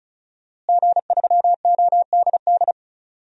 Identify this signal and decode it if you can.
Morse code, heard as a keyed tone, about 34 wpm, 700 Hz. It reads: G3ODD
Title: DXCC-100 Call Signs - 5 Characters - Repeated 3x - 35wpm